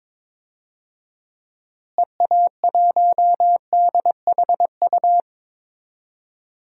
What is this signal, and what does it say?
Morse code, heard as a keyed tone, about 22 wpm, 700 Hz. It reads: EA1DHU
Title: DXCC-100 Call Signs - 6 Characters - Repeated 3x - 22wpm